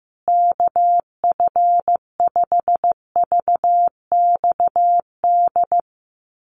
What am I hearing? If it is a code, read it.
Morse code, heard as a keyed tone, about 15 wpm, 700 Hz. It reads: KF5VXD
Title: USA Call Signs Repeated 3x 15wpm